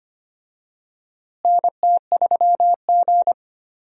Morse code, heard as a keyed tone, about 25 wpm, 700 Hz. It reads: NT3G
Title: USA Call Signs - 4 Characters - Repeated 2x - 25wpm